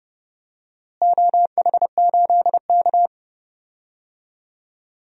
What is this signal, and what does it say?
Morse code, heard as a keyed tone, about 30 wpm, 700 Hz. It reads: OH8K
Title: DXCC-100 Call Signs - 4 Characters - Repeated 3x - 30wpm